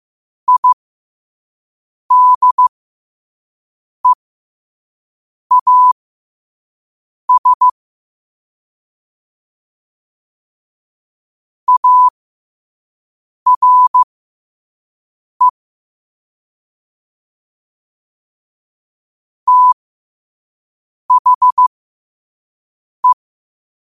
Morse code, heard as a keyed tone, about 15 wpm, spaced as for 5 wpm, 1000 Hz. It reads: IDEAS ARE THE
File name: qotdfiles / 5wpm